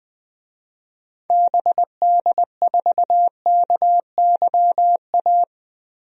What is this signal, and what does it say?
Morse code, heard as a keyed tone, about 20 wpm, 700 Hz. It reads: BD4KYA
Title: DXCC-100 Call Signs - 6 Characters - Repeated 2x - 20wpm